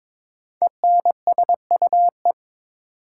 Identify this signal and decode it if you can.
Morse code, heard as a keyed tone, about 22 wpm, 700 Hz. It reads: ENSUE
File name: 5 Letter Words - First 3 Letters Given - 22wpm